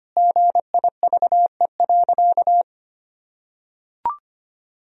Morse code, heard as a keyed tone, about 25 wpm, 700 Hz. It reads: GIVE.
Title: Sentences from Top 1000 Words - Review Part 4 - 25wpm